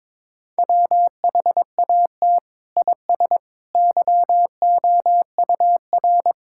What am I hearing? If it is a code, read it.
Morse code, heard as a keyed tone, about 22 wpm, 700 Hz. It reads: WHAT IS YOUR
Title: Sentences from Top 1000 Words - Review Part 6 - 22wpm